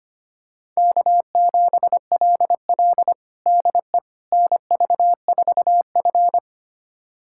Morse code, heard as a keyed tone, about 25 wpm, 700 Hz. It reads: K7LL DE NV4F
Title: USA Callsign DE Callsign - 4 Characters - 25wpm